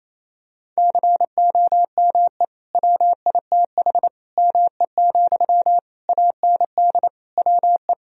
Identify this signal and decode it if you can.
Morse code, heard as a keyed tone, about 28 wpm, 700 Hz. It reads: COME WITH ME, AND WE
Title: Sentences from Top 1000 Words - Review Part 2 - 28wpm